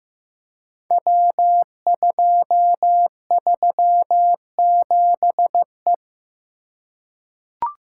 Morse code, heard as a keyed tone, about 15 wpm, 700 Hz. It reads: W237E
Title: Code Groups - 5 Letter:Numbers - 15wpm